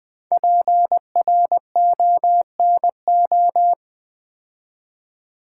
Morse code, heard as a keyed tone, about 20 wpm, 700 Hz. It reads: PRONO
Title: Build A Word - 5 Letters At A Time - 10-Letter Words - 20wpm